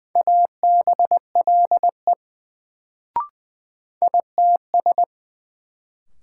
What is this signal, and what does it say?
Morse code, heard as a keyed tone, about 20 wpm, 700 Hz. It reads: ABLE ITS
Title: Sets of 2 Words - New Word Drill - 20wpm